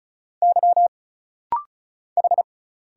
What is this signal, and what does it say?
Morse code, heard as a keyed tone, about 34 wpm, 700 Hz. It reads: Y H
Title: Single Letter-Number Alt - Rapid-Fire 35wpm